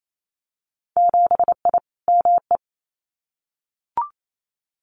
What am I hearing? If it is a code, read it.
Morse code, heard as a keyed tone, about 28 wpm, 700 Hz. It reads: 7I ME
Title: Winter Field Day Exchanges Encore 28wpm